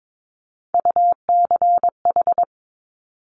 Morse code, heard as a keyed tone, about 22 wpm, 700 Hz. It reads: UCH
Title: Top Three Letter Combinations in Words 22wpm